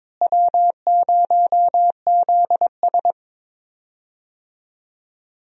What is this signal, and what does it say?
Morse code, heard as a keyed tone, about 22 wpm, 700 Hz. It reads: W0ZS
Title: USA Call Signs - 4 Characters - Repeated 3x - 22wpm